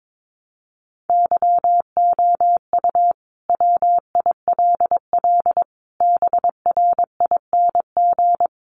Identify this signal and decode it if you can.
Morse code, heard as a keyed tone, about 22 wpm, 700 Hz. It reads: YOU WILL BRING
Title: Sentences from Top 1300 Words 22wpm